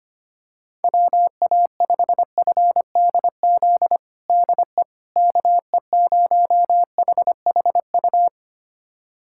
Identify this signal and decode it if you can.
Morse code, heard as a keyed tone, about 25 wpm, 700 Hz. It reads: WA5FDZ DE KE0HHU
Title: USA Callsign DE Callsign 25wpm